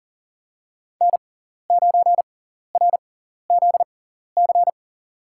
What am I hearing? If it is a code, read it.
Morse code, heard as a keyed tone, about 40 wpm, spaced as for 14 wpm, 700 Hz. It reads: N9RZC
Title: USA Call Signs - 5 Characters - 6x Character Spacing - 40wpm